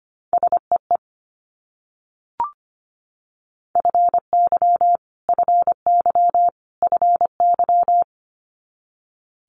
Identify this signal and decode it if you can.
Morse code, heard as a keyed tone, about 25 wpm, 700 Hz. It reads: SEE FY FY FY
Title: Multi-letter Phonemes Repeated 3x 25wpm